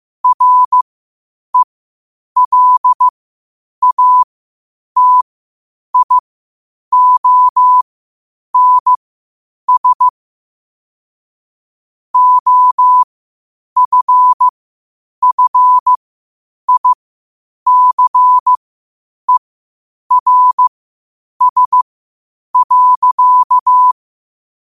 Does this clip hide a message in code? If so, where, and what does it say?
Yes: Morse code from 0.2 to 23.9 s, heard as a keyed tone, about 15 wpm, spaced as for 8 wpm, 1000 Hz. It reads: RELATIONS OFFICERS.